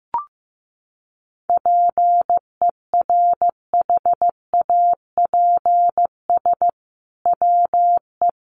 Morse code, heard as a keyed tone, about 15 wpm, 700 Hz. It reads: PERHAPS WE
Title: Sentences from Top 1500 Words 15wpm